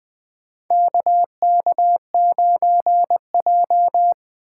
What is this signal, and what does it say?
Morse code, heard as a keyed tone, about 20 wpm, 700 Hz. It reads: KK9J
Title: USA Call Signs - 4 Characters - 4x Character Spacing - 20wpm